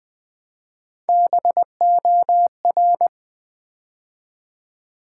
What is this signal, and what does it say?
Morse code, heard as a keyed tone, about 20 wpm, 700 Hz. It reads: BOR
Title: Build A Word - 3 Letters At A Time - 6-Letter Words - 20wpm